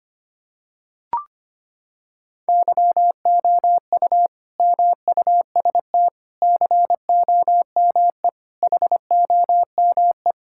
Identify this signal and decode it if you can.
Morse code, heard as a keyed tone, about 25 wpm, 700 Hz. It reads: YOU MUST COME HOME
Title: Sentences from Top 1000 Words - Review Part 5 - 25wpm